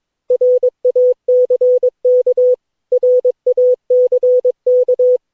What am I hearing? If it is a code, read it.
Morse code, heard as a keyed tone, about 22 wpm, 500 Hz. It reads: RACK RACK
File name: cw / Session9 cw training